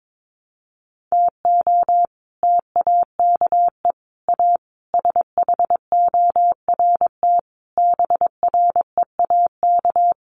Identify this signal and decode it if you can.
Morse code, heard as a keyed tone, about 22 wpm, 700 Hz. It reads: TO TAKE A SHORT BREAK
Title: Sets of 5 Words Encore2 22wpm